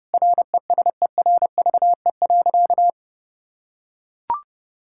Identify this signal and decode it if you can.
Morse code, heard as a keyed tone, about 32 wpm, 700 Hz. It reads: RESERVE.
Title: Sentences from Top 1000 Words - Review Part 6 - 30wpm